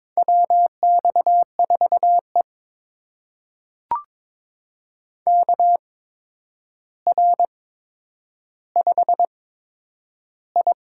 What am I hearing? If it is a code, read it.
Morse code, heard as a keyed tone, about 22 wpm, spaced as for 9 wpm, 700 Hz. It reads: WX4E KR5I
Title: USA Call Signs - 4 Characters - 8x Character Spacing - 22wpm